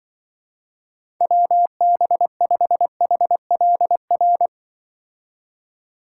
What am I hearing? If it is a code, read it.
Morse code, heard as a keyed tone, about 24 wpm, 700 Hz. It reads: WB5HLR
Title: Speed Racing - USA 2x3 Call Signs 20wpm